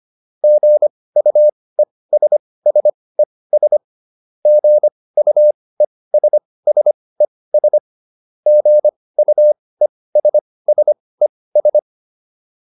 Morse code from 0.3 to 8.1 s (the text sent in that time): GUESSES GUESSES